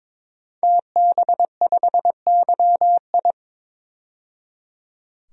Morse code, heard as a keyed tone, about 22 wpm, 700 Hz. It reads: TB5YI
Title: DXCC-100 Call Signs - 5 Characters - 22wpm